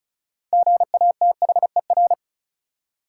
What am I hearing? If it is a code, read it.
Morse code, heard as a keyed tone, about 36 wpm, 700 Hz. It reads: GATHER
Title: Six Letter Words - 35wpm